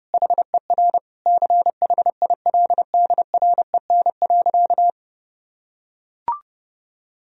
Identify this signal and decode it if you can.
Morse code, heard as a keyed tone, about 30 wpm, 700 Hz. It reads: HER CHILDREN.
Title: Sentences from Top 1000 Words - Review Part 3 - 30wpm